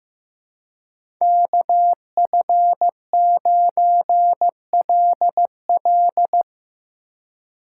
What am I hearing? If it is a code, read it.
Morse code, heard as a keyed tone, about 17 wpm, 700 Hz. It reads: KF9LL
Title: Speed Racing - USA 2x2 Call Signs 15wpm